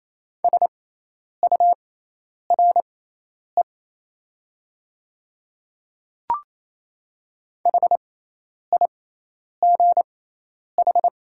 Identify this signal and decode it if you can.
Morse code, heard as a keyed tone, about 28 wpm, spaced as for 10 wpm, 700 Hz. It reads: SURE HIGH
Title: 6x Character Spacing - Top 500 Words - 28wpm